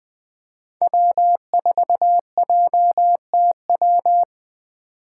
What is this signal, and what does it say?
Morse code, heard as a keyed tone, about 21 wpm, 700 Hz. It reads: W4JTW
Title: USA Call Signs - 8x Character Spacing - 20wpm